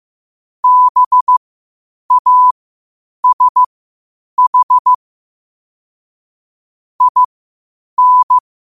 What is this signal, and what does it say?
Morse code, heard as a keyed tone, about 15 wpm, spaced as for 8 wpm, 1000 Hz. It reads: BASH IN
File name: qotdfiles / 8wpm